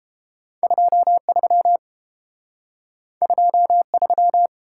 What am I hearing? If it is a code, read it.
Morse code, heard as a keyed tone, about 29 wpm, 700 Hz. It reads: 23 23